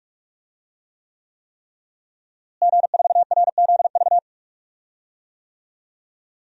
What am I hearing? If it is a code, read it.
Morse code, heard as a keyed tone, about 46 wpm, 700 Hz. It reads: G4RZV
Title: DXCC-100 Call Signs - 5 Characters - Repeated 3x - 45wpm